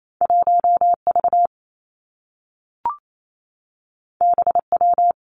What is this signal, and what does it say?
Morse code, heard as a keyed tone, about 28 wpm, 700 Hz. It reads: 1 V B W
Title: Two Challenging Letter-Number Combinations 28wpm